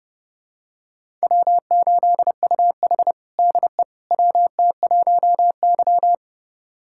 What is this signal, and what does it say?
Morse code, heard as a keyed tone, about 30 wpm, 700 Hz. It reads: W8UH DE WT1Y
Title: USA Callsign DE Callsign - 4 Characters - 30wpm